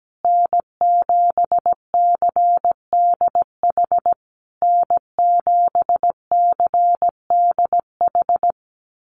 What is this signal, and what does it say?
Morse code, heard as a keyed tone, about 17 wpm, 700 Hz. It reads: N7CDH N7CDH
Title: USA Call Signs Repeated 3x 17wpm